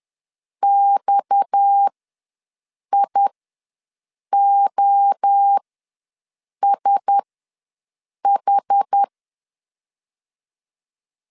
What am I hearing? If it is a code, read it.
Morse code, heard as a keyed tone, about 10 wpm, spaced as for 6 wpm, 800 Hz. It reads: XIOSH
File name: LEZIONI / 7 GF9Z / 7 GF9 ws 10wpm 4034